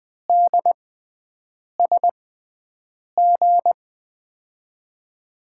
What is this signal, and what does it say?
Morse code, heard as a keyed tone, about 20 wpm, spaced as for 7 wpm, 700 Hz. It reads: DSG
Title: USA Call Signs - 5 Characters - 6x Character Spacing - 20wpm